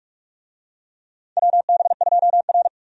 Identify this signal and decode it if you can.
Morse code, heard as a keyed tone, about 47 wpm, 700 Hz. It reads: WB2R
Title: USA Call Signs - 4 Characters - Repeated 2x - 45wpm